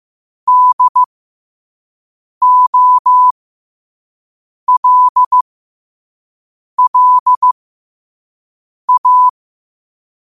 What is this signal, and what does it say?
Morse code, heard as a keyed tone, about 15 wpm, spaced as for 5 wpm, 1000 Hz. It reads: DOLLA
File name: qotdfiles / 5wpm